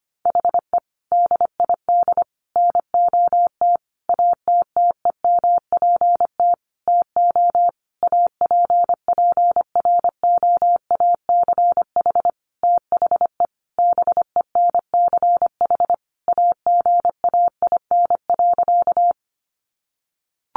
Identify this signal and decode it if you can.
Morse code, heard as a keyed tone, about 25 wpm, 700 Hz. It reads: HE DID NOT ATTEMPT TO APPROACH THE BENCH AGAIN.
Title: Sentences from Top 1400 Words 25wpm